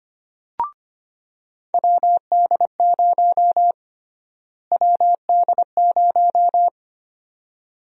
Morse code, heard as a keyed tone, about 25 wpm, 700 Hz. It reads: WD0 WD0
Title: USA Call Sign Prefixes - Repeated 2x - 25wpm